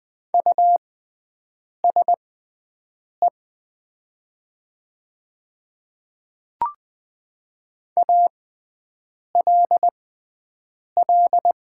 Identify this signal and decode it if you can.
Morse code, heard as a keyed tone, about 20 wpm, spaced as for 7 wpm, 700 Hz. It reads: USE ALL
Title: 6x Character Spacing - Top 100 Words - 20wpm